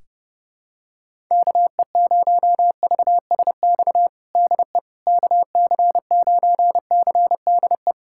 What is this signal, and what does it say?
Morse code, heard as a keyed tone, about 30 wpm, 700 Hz. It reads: KE0VSX DE KC9CDE
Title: USA Callsign DE Callsign 30wpm